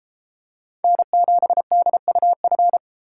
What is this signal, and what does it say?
Morse code, heard as a keyed tone, about 33 wpm, 700 Hz. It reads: N7DUF